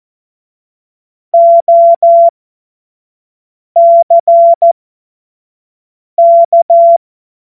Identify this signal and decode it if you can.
Morse code, heard as a keyed tone, about 14 wpm, spaced as for 5 wpm, 680 Hz. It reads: OCK
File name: Morsekurs / Witze / Witz 12. Oktober 2021, 25 F70